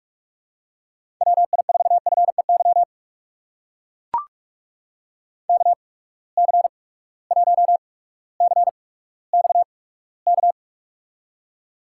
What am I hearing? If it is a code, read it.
Morse code, heard as a keyed tone, about 44 wpm, spaced as for 19 wpm, 700 Hz. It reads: WI4FEY KC1CXK
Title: USA Call Signs - 6 Characters - 8x Character Spacing - 45wpm